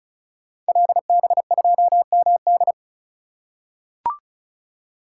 Morse code, heard as a keyed tone, about 36 wpm, 700 Hz. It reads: LB2MD
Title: Code Groups - 5 Letter:Numbers - 35wpm